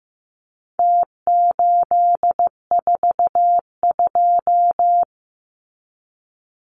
Morse code, heard as a keyed tone, about 15 wpm, 700 Hz. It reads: T842
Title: Numbers - Contest Style - 4-Digit 15wpm